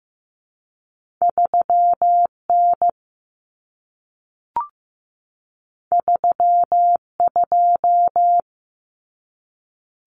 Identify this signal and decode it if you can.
Morse code, heard as a keyed tone, about 15 wpm, 700 Hz. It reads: 3N 32
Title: Common Cut Numbers - 2-Digit 15wpm